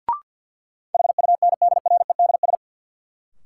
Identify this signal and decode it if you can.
Morse code, heard as a keyed tone, about 52 wpm, 700 Hz. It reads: HUNDREDS
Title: Sets of 4 Words - New Word Drill - 50wpm